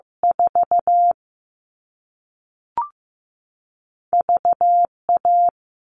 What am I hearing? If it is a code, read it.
Morse code, heard as a keyed tone, about 15 wpm, 700 Hz. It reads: 4 V A
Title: All Cut Numbers - 2-Digit 15wpm